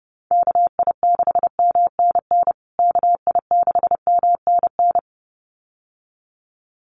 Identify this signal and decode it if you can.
Morse code, heard as a keyed tone, about 30 wpm, 700 Hz. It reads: KI6MNN KI6MNN
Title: USA Call Signs Repeated 2x 30wpm